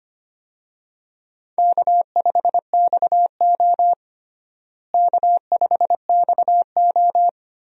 Morse code, heard as a keyed tone, about 25 wpm, 700 Hz. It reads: K5XO K5XO
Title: USA Call Signs - 4 Characters - Repeated 3x - 25wpm